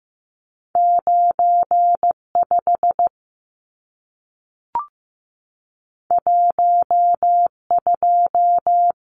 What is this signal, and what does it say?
Morse code, heard as a keyed tone, about 15 wpm, 700 Hz. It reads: 95 12
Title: Numbers - 2-Digit 15wpm